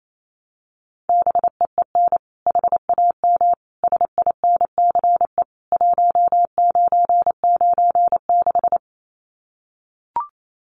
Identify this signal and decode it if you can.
Morse code, heard as a keyed tone, about 28 wpm, 700 Hz. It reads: BEEN HAM SINCE 1996
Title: Common QSO Phrases 28wpm